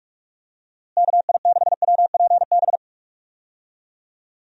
Morse code, heard as a keyed tone, about 41 wpm, 700 Hz. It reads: KI6WPB
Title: USA Call Signs - 6 Characters - Repeated 3x - 45wpm